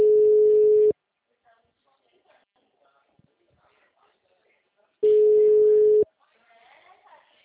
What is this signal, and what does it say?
Ringback tone (Europe, 425 Hz), with speech.
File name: out- -2006- - - .63976